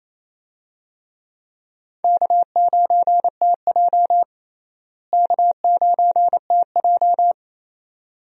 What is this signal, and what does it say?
Morse code, heard as a keyed tone, about 28 wpm, 700 Hz. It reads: K9TJ K9TJ
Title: USA Call Signs - 4 Characters - Repeated 2x - 28wpm